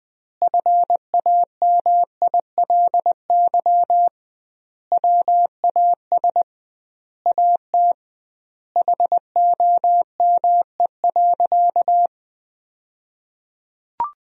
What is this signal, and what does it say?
Morse code, heard as a keyed tone, about 21 wpm, 700 Hz. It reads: FAMILY WAS AT HOME.
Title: 2x Word-Spacing - Sentences from Top 400 Words - 20wpm